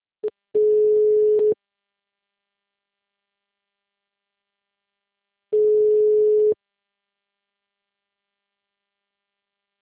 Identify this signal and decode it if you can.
Ringback tone (Europe, 425 Hz).